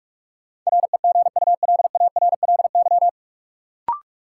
Morse code, heard as a keyed tone, about 46 wpm, 700 Hz. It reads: REGULARLY
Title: Sentences from Top 1400 Words - New Word Drill Alt - 45wpm